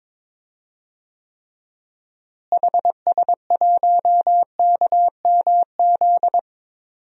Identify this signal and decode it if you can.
Morse code, heard as a keyed tone, about 22 wpm, 700 Hz. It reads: HS1KMZ
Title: DXCC-100 Call Signs - 6 Characters - Repeated 2x - 22wpm